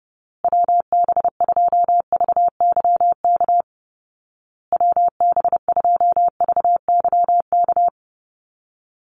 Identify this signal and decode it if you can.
Morse code, heard as a keyed tone, about 30 wpm, 700 Hz. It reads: WB2VYK WB2VYK
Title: Call Signs Repeated 2x - 30wpm